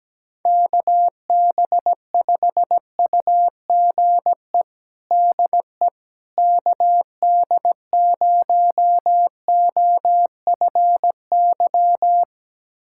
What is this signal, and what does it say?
Morse code, heard as a keyed tone, about 17 wpm, 700 Hz. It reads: KB5UGE DE KD0OFY
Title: USA Callsign DE Callsign 17wpm